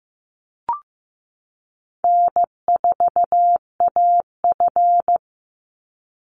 Morse code, heard as a keyed tone, about 15 wpm, 700 Hz. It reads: N4AF
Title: Call Signs Repeated 2x - 15wpm